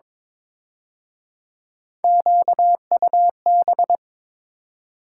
Morse code, heard as a keyed tone, about 26 wpm, 700 Hz. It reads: QUB
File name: Speed-Racing - Three Letter-Number Combinations - 22wpm